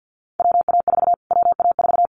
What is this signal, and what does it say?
Morse code, heard as a keyed tone, about 50 wpm, 700 Hz. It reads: WA4 WA4
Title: Callsign Prefixes for Top 25 DXCC - Repeated 3x - 50wpm